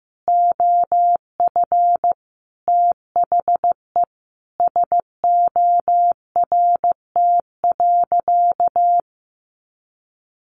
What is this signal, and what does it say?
Morse code, heard as a keyed tone, about 15 wpm, 700 Hz. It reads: OF THE SORT.
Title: Sentences from Top 1300 Words 15wpm